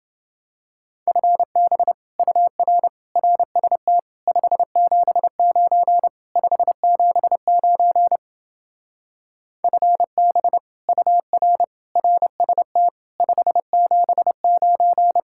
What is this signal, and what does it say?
Morse code, heard as a keyed tone, about 29 wpm, 700 Hz. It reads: FB UR RST 579 579 FB UR RST 579